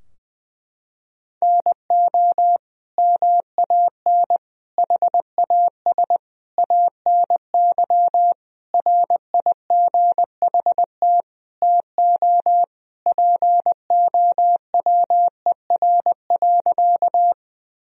Morse code, heard as a keyed tone, about 20 wpm, 700 Hz. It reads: NO MAN HAS ANY RIGHT TO POWER.
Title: Sentences from Top 1000 Words - Review Part 2 - 20wpm